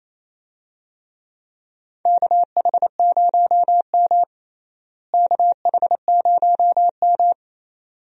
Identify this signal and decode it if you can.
Morse code, heard as a keyed tone, about 28 wpm, 700 Hz. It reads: KH0M KH0M
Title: USA Call Signs - 4 Characters - Repeated 2x - 28wpm